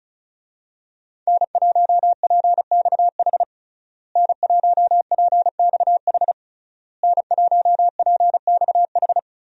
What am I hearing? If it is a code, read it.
Morse code, heard as a keyed tone, about 36 wpm, 700 Hz. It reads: N1PXH N1PXH N1PXH
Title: USA Call Signs - 5 Characters - Repeated 3x - 35wpm